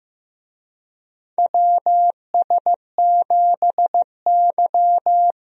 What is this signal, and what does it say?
Morse code, heard as a keyed tone, about 15 wpm, 700 Hz. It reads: WS7Y
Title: USA Call Signs - 4 Characters - Repeated 2x - 15wpm